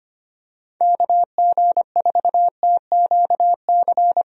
Morse code, heard as a keyed tone, about 25 wpm, 700 Hz. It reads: KG4TQC
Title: USA Call Signs - 6 Characters - Repeated 2x - 25wpm